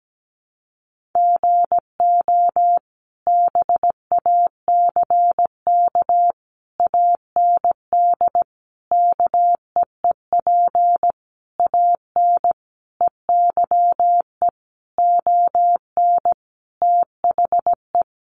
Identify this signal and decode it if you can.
Morse code, heard as a keyed tone, about 17 wpm, 700 Hz. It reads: GO BACK AND KEEP AN EYE ON THE
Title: Sentences from Top 300 Words 17wpm